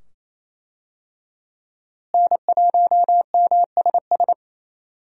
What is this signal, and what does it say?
Morse code, heard as a keyed tone, about 28 wpm, 700 Hz. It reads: N1MSS
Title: USA Call Signs - 5 Characters - Repeated 2x - 28wpm